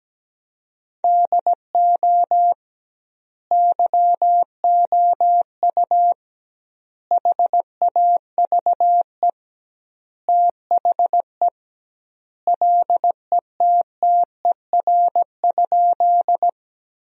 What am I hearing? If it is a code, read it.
Morse code, heard as a keyed tone, about 17 wpm, 700 Hz. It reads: DO YOU HAVE THE LETTER?
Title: 2x Word-Spacing - Sentences from Top 300 Words - 17wpm